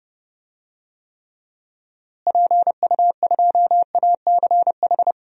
Morse code, heard as a keyed tone, about 30 wpm, 700 Hz. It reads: PU2ACH
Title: DXCC-100 Call Signs - 6 Characters - Repeated 3x - 30wpm